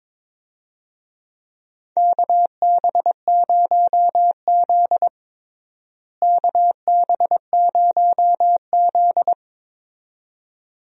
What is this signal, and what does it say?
Morse code, heard as a keyed tone, about 22 wpm, 700 Hz. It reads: KB0Z KB0Z
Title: USA Call Signs - 4 Characters - Repeated 2x - 22wpm